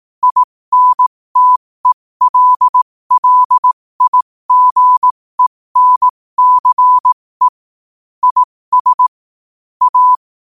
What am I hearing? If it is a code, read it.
Morse code, heard as a keyed tone, about 18 wpm, spaced as for 15 wpm, 1000 Hz. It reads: INTELLIGENCE IS A